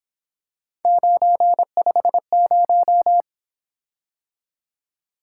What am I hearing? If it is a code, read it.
Morse code, heard as a keyed tone, about 25 wpm, 700 Hz. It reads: 950
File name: Speed-Racing - Three Letter-Number Combinations - 22wpm